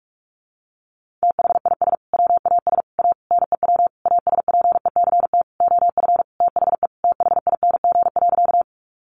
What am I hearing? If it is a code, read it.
Morse code, heard as a keyed tone, about 44 wpm, 700 Hz. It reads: THIS WAS A NEW ASPECT OF THE THING.
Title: Sentences from Top 1300 Words 45wpm